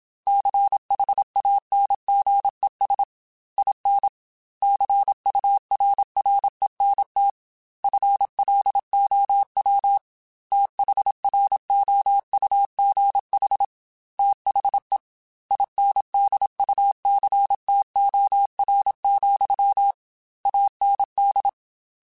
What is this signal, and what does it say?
Morse code, heard as a keyed tone, about 27 wpm, 800 Hz. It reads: CHANGES IN CURRENT FLOW THROUGH THE INDUCTOR, AND
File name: Technical Subjects / Technical Subjects-mp3s / Reactance @25